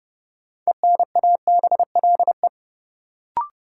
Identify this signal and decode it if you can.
Morse code, heard as a keyed tone, about 31 wpm, 700 Hz. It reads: ENABLE
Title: Sentences from Top 1500 Words - New Word Drill Alt - 30wpm